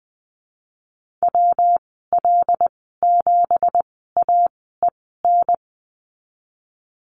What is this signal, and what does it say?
Morse code, heard as a keyed tone, about 20 wpm, spaced as for 15 wpm, 700 Hz. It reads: WL7AEN
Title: 167 Lesson 27g - Number 7 - Call Signs 20wpm